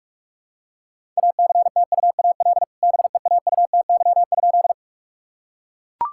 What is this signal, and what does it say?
Morse code, heard as a keyed tone, about 42 wpm, 700 Hz. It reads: ACTUAL BEAUTY?
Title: Sentences from Top 1000 Words - Review Part 5 - 45wpm